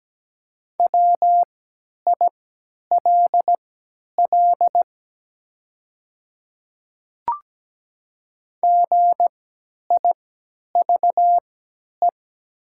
Morse code, heard as a keyed tone, about 17 wpm, spaced as for 10 wpm, 700 Hz. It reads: WILL GIVE
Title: 3x Character Spacing - Top 200 Words - 17wpm